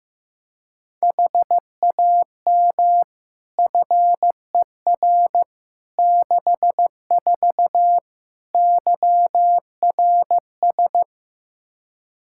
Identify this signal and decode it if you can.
Morse code, heard as a keyed tone, about 15 wpm, 700 Hz. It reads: HAM FER 64 YRS
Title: K1USN SST QSO Practice 15wpm